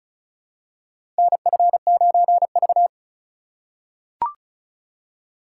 Morse code, heard as a keyed tone, about 36 wpm, 700 Hz. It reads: NF9V
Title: USA Call Signs - 4 Characters - 4x Character Spacing - 35wpm